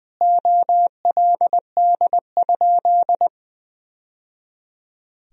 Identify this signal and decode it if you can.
Morse code, heard as a keyed tone, about 20 wpm, 700 Hz. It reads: OLD?
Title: Sentences from Top 1000 Words - Review Part 1 - 20wpm